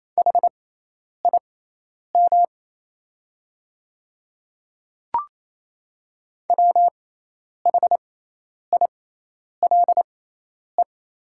Morse code, heard as a keyed tone, about 27 wpm, spaced as for 10 wpm, 700 Hz. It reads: HIM WHILE
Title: 6x Character Spacing - Top 300 Words - 28wpm